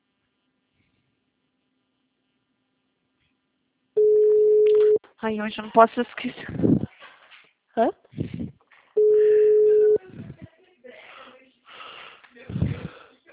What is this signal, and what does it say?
Ringback tone (Europe, 425 Hz), with speech.
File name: out- -2024- - - .8